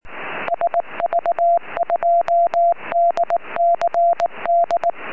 Morse code, heard as a keyed tone, about 19 wpm, 690 Hz. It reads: SV2DCD
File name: sv2dcd 4m